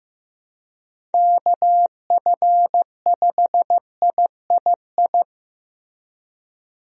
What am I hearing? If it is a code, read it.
Morse code, heard as a keyed tone, about 15 wpm, 700 Hz. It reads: KF5III
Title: USA Call Signs - 6 Characters - Repeated 3x - 15wpm